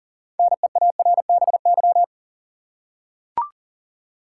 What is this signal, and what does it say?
Morse code, heard as a keyed tone, about 40 wpm, 700 Hz. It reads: NEARBY
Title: Six Letter Words - 40wpm